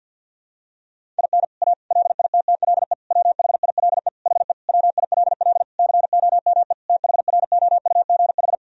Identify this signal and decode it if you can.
Morse code, heard as a keyed tone, about 51 wpm, 700 Hz. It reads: IN A LITTLE WHILE HE WILL COME THROUGH
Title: Sentences from Top 1000 Words - Review Part 1 - 50wpm